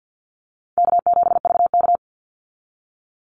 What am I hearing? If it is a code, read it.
Morse code, heard as a keyed tone, about 50 wpm, 700 Hz. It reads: K7VK
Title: NA SOTA Chaser Callsigns 50wpm